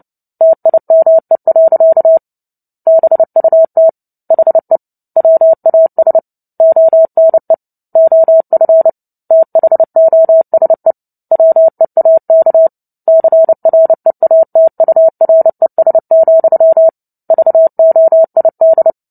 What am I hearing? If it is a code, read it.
Morse code, heard as a keyed tone, about 30 wpm, 650 Hz. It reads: TIME. BUT HE WAS ONE OF THOSE WEAK CREATURES, VOID
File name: Book Two / 03-Chapter 3 - 29 WPM